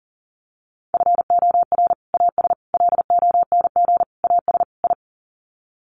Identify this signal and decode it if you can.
Morse code, heard as a keyed tone, about 37 wpm, 700 Hz. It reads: FOR AS LONG AS I
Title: Sets of 5 Words Encore2 40wpm